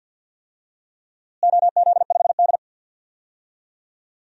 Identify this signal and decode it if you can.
Morse code, heard as a keyed tone, about 47 wpm, 700 Hz. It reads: OZ5D
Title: DXCC-100 Call Signs - 4 Characters - Repeated 2x - 50wpm